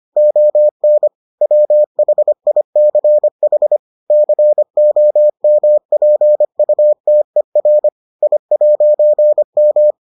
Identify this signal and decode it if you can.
Morse code, heard as a keyed tone, about 25 wpm, 600 Hz. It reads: ON WHICH COMPUTER I'M